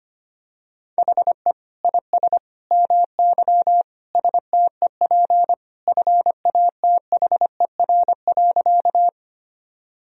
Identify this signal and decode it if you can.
Morse code, heard as a keyed tone, about 25 wpm, 700 Hz. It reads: HE IS MY STEP FATHER.
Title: Sentences from Top 1000 Words - Review Part 2 - 25wpm